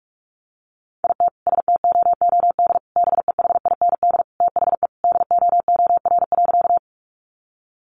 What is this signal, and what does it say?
Morse code, heard as a keyed tone, about 44 wpm, 700 Hz. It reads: IT STOOD BEHIND THE DOOR.
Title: Sentences from Top 500 Words 45wpm